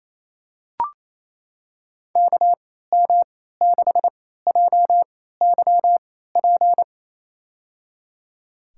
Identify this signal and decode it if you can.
Morse code, heard as a keyed tone, about 27 wpm, spaced as for 16 wpm, 700 Hz. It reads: KM6JYP
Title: USA Call Signs - 6 Characters - 3x Character Spacing - 28wpm